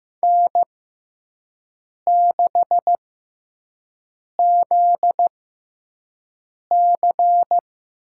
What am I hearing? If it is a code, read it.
Morse code, heard as a keyed tone, about 15 wpm, spaced as for 5 wpm, 700 Hz. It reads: N6ZC
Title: USA Call Signs - 5 Characters - 6x Character Spacing - 15wpm